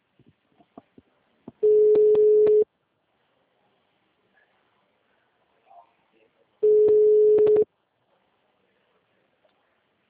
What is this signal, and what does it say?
Ringback tone (Europe, 425 Hz), with speech.
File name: out- -2005- - - .70889